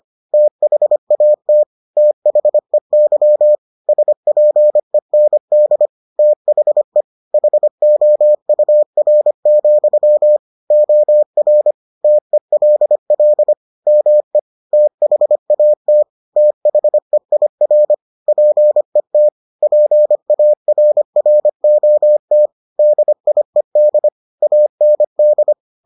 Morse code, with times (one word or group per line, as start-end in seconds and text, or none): 0.3-1.6 THAT
2.0-3.5 THEY
3.9-5.9 SPEND
6.2-7.0 THE
7.3-10.4 HOUR,
10.7-11.7 OR
12.0-13.5 TELL
13.9-14.4 ME
14.7-16.0 THAT
16.4-17.9 THEIR
18.3-19.3 PET
19.6-22.5 PARROT
22.8-24.1 DIED
24.4-25.5 AND